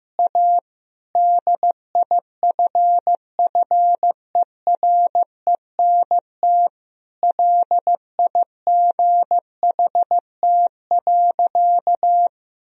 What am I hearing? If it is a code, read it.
Morse code, heard as a keyed tone, about 15 wpm, 700 Hz. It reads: A DIFFERENT LIGHT.
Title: Sentences from Top 1000 Words - Review Part 3 - 15wpm